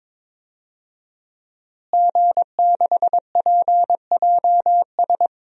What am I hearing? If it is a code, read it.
Morse code, heard as a keyed tone, about 22 wpm, 700 Hz. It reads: G6PJS
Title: DXCC-100 Call Signs - 5 Characters - Repeated 2x - 22wpm